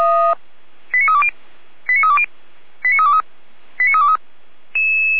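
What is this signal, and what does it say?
Five-tone selective call (ZVEI), heard as quick runs of five tones, tones of about 70 ms. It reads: 89320 89320 89323 89323
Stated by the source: ZVEI five-tone selective call received by SDR.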